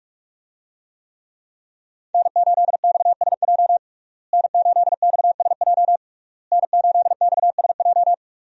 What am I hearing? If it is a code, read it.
Morse code, heard as a keyed tone, about 46 wpm, 700 Hz. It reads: N8XSJ N8XSJ N8XSJ
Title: USA Call Signs - 5 Characters - Repeated 3x - 45wpm